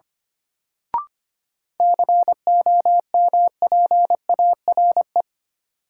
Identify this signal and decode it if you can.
Morse code, heard as a keyed tone, about 25 wpm, 700 Hz. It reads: COMPARE
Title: Sentences from Top 1300 Words - New Word Drill Alt - 25wpm